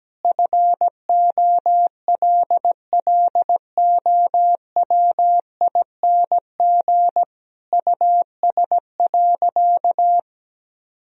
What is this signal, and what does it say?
Morse code, heard as a keyed tone, about 17 wpm, 700 Hz. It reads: FOLLOWING US.
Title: Sentences from Top 1000 Words - Review Part 4 - 17wpm